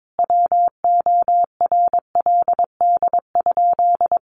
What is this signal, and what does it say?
Morse code, heard as a keyed tone, about 22 wpm, 700 Hz. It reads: WORLD?
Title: Sentences from Top 200 Words 22wpm